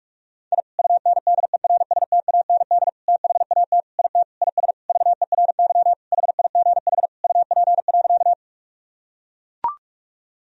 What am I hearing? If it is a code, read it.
Morse code, heard as a keyed tone, about 43 wpm, 700 Hz. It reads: I UNDERSTAND THAT IT IS VERY HIGH UP.
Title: Sentences from Top 1000 Words - Review Part 1 - 45wpm